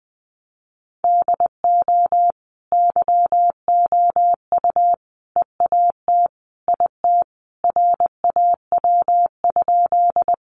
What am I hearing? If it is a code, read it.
Morse code, heard as a keyed tone, about 20 wpm, 700 Hz. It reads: DO YOU EAT IT RAW?
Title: Sentences from Top 1500 Words 20wpm